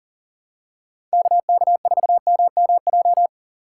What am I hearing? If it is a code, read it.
Morse code, heard as a keyed tone, about 37 wpm, 700 Hz. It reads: KK4MMJ
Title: USA Call Signs - 6 Characters - 2x Character Spacing - 40wpm